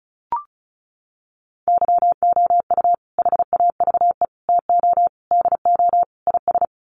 Morse code, heard as a keyed tone, about 34 wpm, 700 Hz. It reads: YOU HAVE TO DO IS
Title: Sets of 5 Words 35wpm